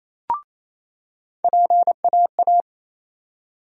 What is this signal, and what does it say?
Morse code, heard as a keyed tone, about 28 wpm, 700 Hz. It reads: PAA
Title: USA Call Sign Postfixes - 3 Letters 28wpm